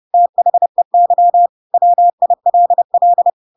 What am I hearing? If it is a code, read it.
Morse code, heard as a keyed tone, about 30 wpm, 700 Hz. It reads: THEY WILL
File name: Pinocchio / Pinocchio ch4 30 30 ik1hge